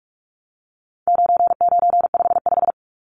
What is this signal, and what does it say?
Morse code, heard as a keyed tone, about 44 wpm, 700 Hz. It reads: 9955
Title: Numbers - 4-Digit 45wpm